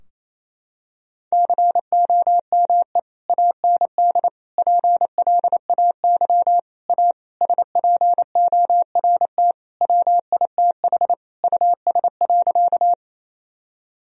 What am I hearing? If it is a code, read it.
Morse code, heard as a keyed tone, about 28 wpm, 700 Hz. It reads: COME AND PLAY A SPORT WITH US.
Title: Sentences from Top 1000 Words - Review Part 4 - 28wpm